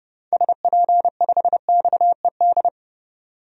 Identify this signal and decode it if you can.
Morse code, heard as a keyed tone, about 30 wpm, 700 Hz. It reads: SP5XED
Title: DXCC-100 Call Signs - 6 Characters - 30wpm